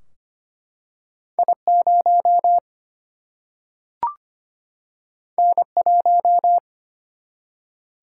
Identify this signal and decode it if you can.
Morse code, heard as a keyed tone, about 25 wpm, 700 Hz. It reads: I0 N1
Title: Single-Letter Single-Number 25wpm